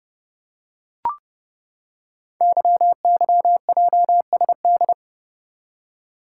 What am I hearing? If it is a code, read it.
Morse code, heard as a keyed tone, about 29 wpm, 700 Hz. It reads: YYJSD
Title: Code Groups - 5 Letters - 30wpm